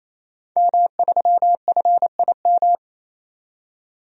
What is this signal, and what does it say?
Morse code, heard as a keyed tone, about 28 wpm, 700 Hz. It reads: M3FIM
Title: DXCC-100 Call Signs - 5 Characters - 28wpm